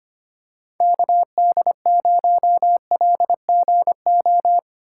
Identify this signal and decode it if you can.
Morse code, heard as a keyed tone, about 25 wpm, 700 Hz. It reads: KD0LGO
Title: USA Call Signs - 6 Characters - 2x Character Spacing - 25wpm